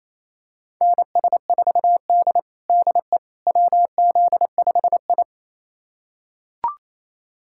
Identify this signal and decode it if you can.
Morse code, heard as a keyed tone, about 28 wpm, 700 Hz. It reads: NS4D DE WZ5I
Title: USA Callsign DE Callsign - 4 Characters - 28wpm